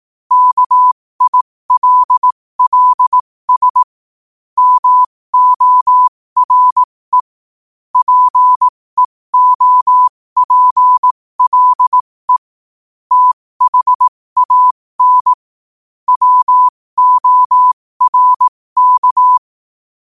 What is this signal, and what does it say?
Morse code, heard as a keyed tone, about 18 wpm, spaced as for 15 wpm, 1000 Hz. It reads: KILLS MORE PEOPLE THAN WORK